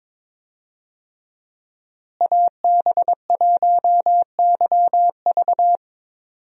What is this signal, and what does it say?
Morse code, heard as a keyed tone, about 22 wpm, 700 Hz. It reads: AB1YV
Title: USA Call Signs - 5 Characters - Repeated 2x - 22wpm